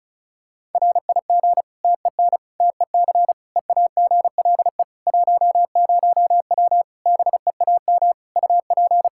Morse code, heard as a keyed tone, about 36 wpm, 700 Hz. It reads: RIG TEN TEC EAGLE 10W BEAM UP
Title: Practice QSO - Exchange 2 - Info - 35wpm